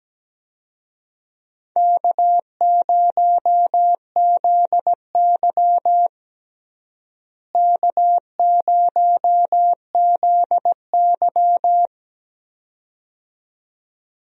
Morse code, heard as a keyed tone, about 17 wpm, 700 Hz. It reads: K0ZY K0ZY
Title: USA Call Signs - 4 Characters - Repeated 2x - 17wpm